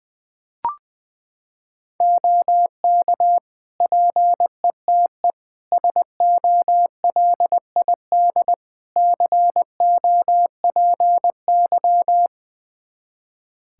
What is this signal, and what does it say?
Morse code, heard as a keyed tone, about 21 wpm, 700 Hz. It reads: OK PETE SOLID COPY
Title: Practice QSO - Exchange 3 - More Info - 20wpm